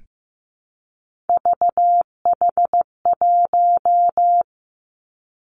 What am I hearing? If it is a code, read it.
Morse code, heard as a keyed tone, about 15 wpm, 700 Hz. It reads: VH1
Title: Three Challenging Letter-Number Combinations 15wpm